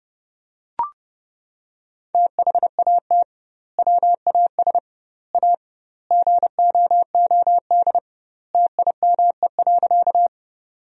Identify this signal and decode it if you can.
Morse code, heard as a keyed tone, about 30 wpm, 700 Hz. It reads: THAT WAS A GOOD TIME.
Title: 2x Word-Spacing - Sentences from Top 200 Words - 30wpm